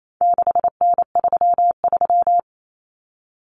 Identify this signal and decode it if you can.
Morse code, heard as a keyed tone, about 28 wpm, 700 Hz. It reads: 6N33
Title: Common Cut Numbers - 4-Digit 28wpm